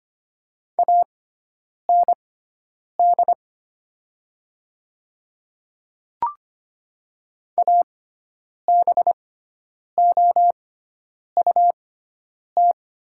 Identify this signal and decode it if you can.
Morse code, heard as a keyed tone, about 25 wpm, spaced as for 9 wpm, 700 Hz. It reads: AND ABOUT
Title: 6x Character Spacing - Top 100 Words - 25wpm